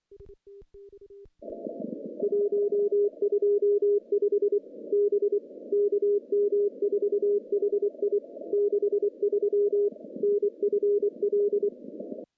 Morse code, heard as a keyed tone, about 24 wpm, 410 Hz. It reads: 125 B KM4HI 63 NFL
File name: contests / CW-SS-k5tr- -1400